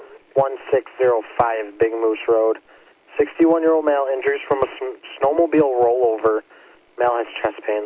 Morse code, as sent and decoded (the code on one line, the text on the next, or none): none